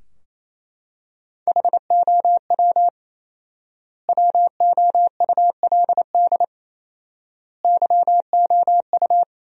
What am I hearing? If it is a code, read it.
Morse code, heard as a keyed tone, about 28 wpm, 700 Hz. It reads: HOW WOULD YOU
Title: 4x Word-Spacing - Sentences from Top 100 Words - 28wpm